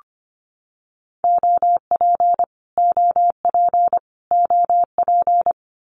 Morse code, heard as a keyed tone, about 25 wpm, 700 Hz. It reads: OP OP OP
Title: Abbreviations Repeated 3x - 25wpm